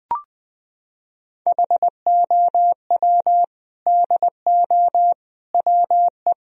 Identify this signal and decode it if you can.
Morse code, heard as a keyed tone, about 20 wpm, 700 Hz. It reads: HOW DO WE
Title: Sentences from Top 1000 Words - Review Part 1 - 20wpm